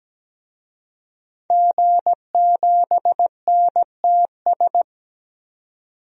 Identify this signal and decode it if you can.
Morse code, heard as a keyed tone, about 17 wpm, 700 Hz. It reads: G7NTS
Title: DXCC-100 Call Signs - 5 Characters - 17wpm